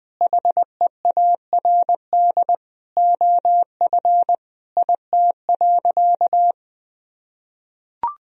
Morse code, heard as a keyed tone, about 20 wpm, 700 Hz. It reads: HEARD OF IT.
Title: Sentences from Top 1000 Words - Review Part 3 - 20wpm